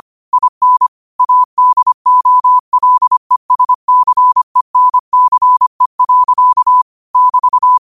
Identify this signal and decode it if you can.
Morse code, heard as a keyed tone, about 25 wpm, 1000 Hz. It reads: IN ADOLESCENCE. <BT>
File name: qotdfiles / 25wpm